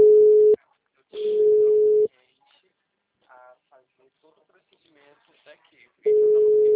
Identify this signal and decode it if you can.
Ringback tone (Europe, 425 Hz), with speech.